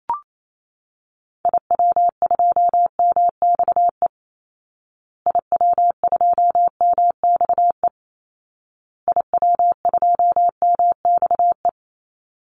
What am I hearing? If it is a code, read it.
Morse code, heard as a keyed tone, about 28 wpm, 700 Hz. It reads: IW2MXE IW2MXE IW2MXE
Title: Call Signs Repeated 2x - 28wpm